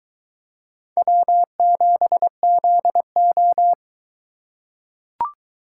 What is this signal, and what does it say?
Morse code, heard as a keyed tone, about 22 wpm, 700 Hz. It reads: W7ZO